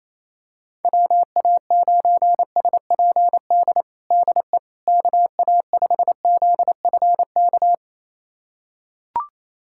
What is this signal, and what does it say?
Morse code, heard as a keyed tone, about 28 wpm, 700 Hz. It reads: WA9SPD DE KA5ZFK
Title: USA Callsign DE Callsign - 6 Characters - 28wpm